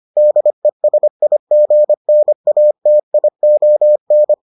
Morse code, heard as a keyed tone, about 25 wpm, 600 Hz. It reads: DESIGNATION